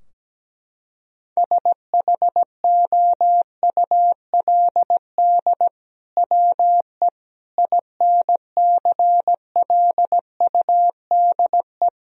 Morse code, heard as a keyed tone, about 17 wpm, 700 Hz. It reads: SHOULD WE INCLUDE
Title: Sentences from Top 1000 Words - Review Part 3 - 17wpm